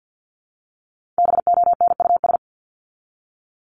Morse code, heard as a keyed tone, about 50 wpm, 700 Hz. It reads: BONUS
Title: Top 1751-2000 Words 50wpm